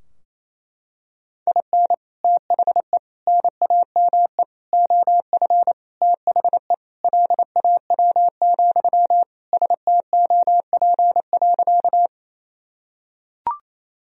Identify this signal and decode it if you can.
Morse code, heard as a keyed tone, about 28 wpm, 700 Hz. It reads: IN THE NAME OF THE LAW, STOP.
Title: Sentences from Top 1000 Words - Review Part 3 - 28wpm